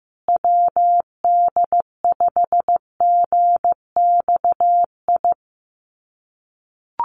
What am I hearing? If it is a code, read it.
Morse code, heard as a keyed tone, about 15 wpm, 700 Hz. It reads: WD5GXI
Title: USA Call Signs Repeated 2x 15wpm